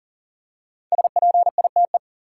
Morse code, heard as a keyed tone, about 40 wpm, 700 Hz. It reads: SPITE
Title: Five Letter Words - 40wpm